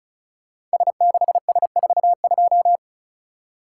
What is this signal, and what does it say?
Morse code, heard as a keyed tone, about 36 wpm, 700 Hz. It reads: S6S42
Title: Code Groups - 5 Letter:Numbers - 35wpm